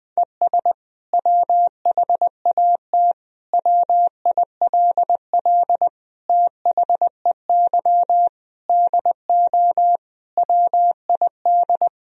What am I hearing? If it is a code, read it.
Morse code, heard as a keyed tone, about 21 wpm, 700 Hz. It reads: ES WHAT WILL THEY DO WID
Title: Sentences from Top 200 Words and Abbreviations 20wpm